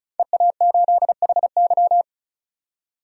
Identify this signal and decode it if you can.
Morse code, heard as a keyed tone, about 34 wpm, 700 Hz. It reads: EA8HY
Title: DXCC-100 Call Signs - 5 Characters - 35wpm